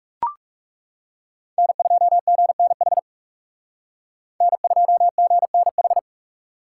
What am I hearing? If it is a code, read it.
Morse code, heard as a keyed tone, about 35 wpm, 700 Hz. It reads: N2GNH N2GNH
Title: Speed Racing - USA 1x3 Call Signs 30wpm